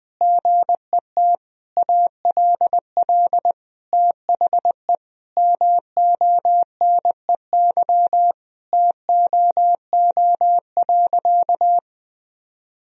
Morse code, heard as a keyed tone, about 21 wpm, 700 Hz. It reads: GET ALL THE MONEY TOO.
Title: Sentences from Top 1000 Words - Review Part 4 - 20wpm